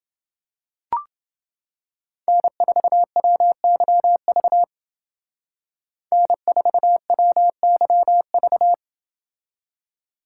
Morse code, heard as a keyed tone, about 29 wpm, 700 Hz. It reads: N4WYV N4WYV